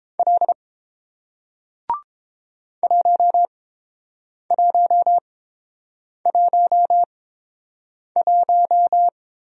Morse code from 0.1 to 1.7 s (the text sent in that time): L